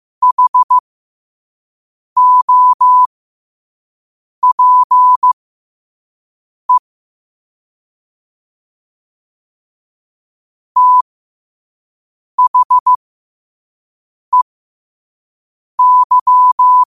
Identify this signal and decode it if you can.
Morse code, heard as a keyed tone, about 15 wpm, spaced as for 5 wpm, 1000 Hz. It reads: HOPE THEY
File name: qotdfiles / 5wpm